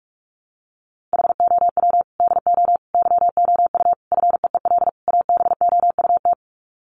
Morse code, heard as a keyed tone, about 44 wpm, 700 Hz. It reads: HOW DO YOU FEEL ABOUT
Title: Sets of 5 Words 45wpm